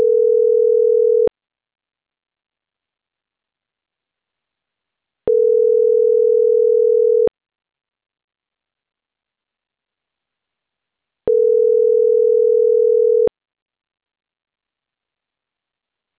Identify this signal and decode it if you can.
Ringback tone (North America), 440 + 480 Hz.